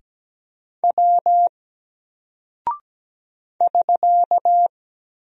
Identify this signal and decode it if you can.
Morse code, heard as a keyed tone, about 17 wpm, 700 Hz. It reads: W <SK>
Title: Single Character (Letter:Number:Punctuation:Prosign) 17wpm